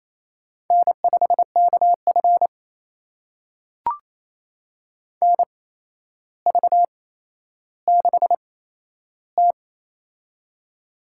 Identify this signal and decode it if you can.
Morse code, heard as a keyed tone, about 28 wpm, spaced as for 12 wpm, 700 Hz. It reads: N5KF NV6T
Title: USA Call Signs - 4 Characters - 8x Character Spacing - 28wpm